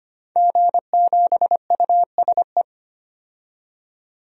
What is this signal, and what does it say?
Morse code, heard as a keyed tone, about 25 wpm, 700 Hz. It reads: G7USE
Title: DXCC-100 Call Signs - 5 Characters - 25wpm